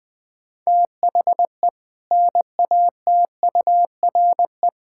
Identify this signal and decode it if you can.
Morse code, heard as a keyed tone, about 20 wpm, 700 Hz. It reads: THE NATURE
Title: Build A Phrase - 4-Word Phrases - 20wpm